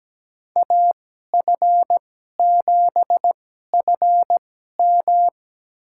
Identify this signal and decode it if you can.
Morse code, heard as a keyed tone, about 17 wpm, spaced as for 12 wpm, 700 Hz. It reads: AF7FM
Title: USA Call Signs - 5 Characters - 2x Character Spacing - 17wpm